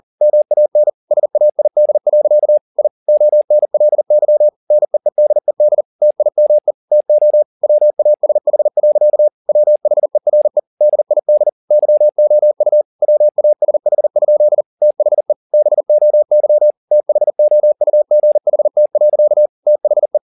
Morse code, heard as a keyed tone, about 40 wpm, 600 Hz. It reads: MAN SAID. I ONLY NEEDED TIME TO WASH. WHERE DID YOU WASH? THE BOY THOUGHT. THE